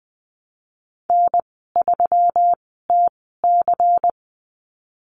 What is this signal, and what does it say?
Morse code, heard as a keyed tone, about 20 wpm, spaced as for 15 wpm, 700 Hz. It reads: N3TC
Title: 147 Lesson 24g - Letter y - Call Signs 20wpm